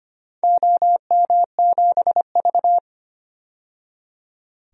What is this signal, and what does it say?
Morse code, heard as a keyed tone, about 25 wpm, 700 Hz. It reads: OM7V
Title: DXCC-100 Call Signs - 4 Characters - 25wpm